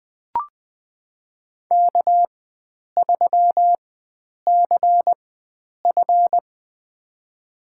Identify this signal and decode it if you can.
Morse code, heard as a keyed tone, about 20 wpm, spaced as for 9 wpm, 700 Hz. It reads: K3CF
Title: USA Call Signs - 4 Characters - 4x Character Spacing - 20wpm